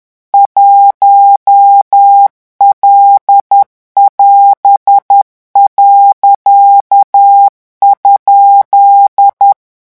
Morse code, heard as a keyed tone, about 10 wpm, 800 Hz. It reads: 1L<AS>.?
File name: morse mp3 / revision16-vitesse 10 mots